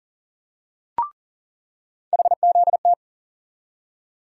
Morse code, heard as a keyed tone, about 40 wpm, 700 Hz. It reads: HZT
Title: USA Call Sign Postfixes - 3 Letters 40wpm